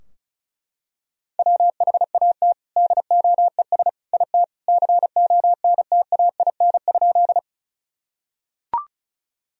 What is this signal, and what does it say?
Morse code, heard as a keyed tone, about 36 wpm, 700 Hz. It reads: WHAT DOES IT CONTAIN?
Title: Sentences from Top 1000 Words - Review Part 2 - 35wpm